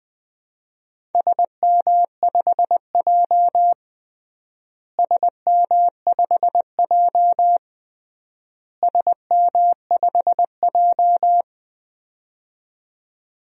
Morse code, heard as a keyed tone, about 20 wpm, 700 Hz. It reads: SM5J SM5J SM5J
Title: DXCC-100 Call Signs - 4 Characters - Repeated 3x - 20wpm